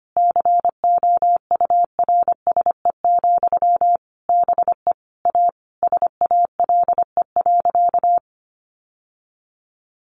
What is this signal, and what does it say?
Morse code, heard as a keyed tone, about 25 wpm, 700 Hz. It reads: COURSE, BE A SALE.
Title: Sentences from Top 1400 Words 25wpm